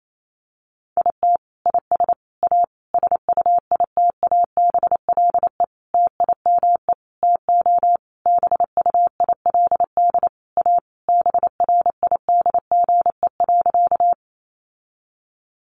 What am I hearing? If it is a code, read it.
Morse code, heard as a keyed tone, about 28 wpm, 700 Hz. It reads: IT IS A SUITABLE TIME TO BUILD A BRIDGE.
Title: Sentences from Top 1400 Words 28wpm